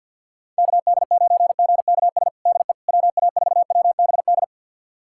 Morse code, heard as a keyed tone, about 48 wpm, 700 Hz. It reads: KD9GKS DE WA4WBD
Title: USA Callsign DE Callsign - 6 Characters - 50wpm